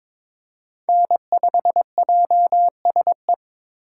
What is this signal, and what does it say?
Morse code, heard as a keyed tone, about 26 wpm, 700 Hz. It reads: N5JSE
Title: Speed Racing - USA 1x3 Call Signs 22wpm